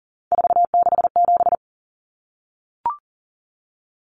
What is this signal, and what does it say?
Morse code, heard as a keyed tone, about 40 wpm, 700 Hz. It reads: 467
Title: Numbers - 3-Digit 40wpm